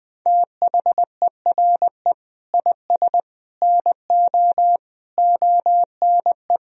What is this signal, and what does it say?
Morse code, heard as a keyed tone, about 20 wpm, 700 Hz. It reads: THERE IS NO ONE
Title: Sentences from Top 500 Words and Abbreviations 20wpm